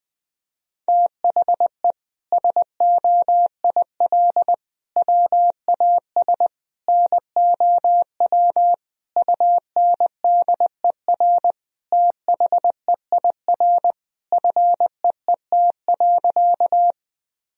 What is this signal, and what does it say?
Morse code, heard as a keyed tone, about 20 wpm, 700 Hz. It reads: THE SOIL WAS NOW UNDER THEIR FEET.
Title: Sentences from Top 1000 Words - Review Part 4 - 20wpm